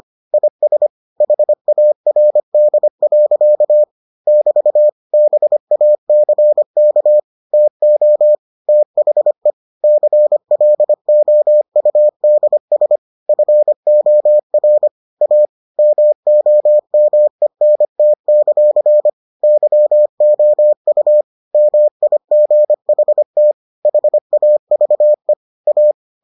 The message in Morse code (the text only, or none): IS HARD. <BT> BACK TO THE CLOUDS FOR A MOMENT; YOU MIGHT HAVE A